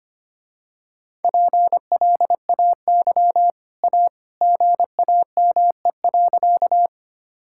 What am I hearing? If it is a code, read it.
Morse code, heard as a keyed tone, about 25 wpm, 700 Hz. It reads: PLAY A GAME.
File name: Easy Sentences with Hints - 25wpm